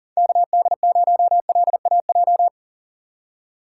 Morse code, heard as a keyed tone, about 40 wpm, 700 Hz. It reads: KD0LAJ
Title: USA Call Signs - 3x Character Spacing - 40wpm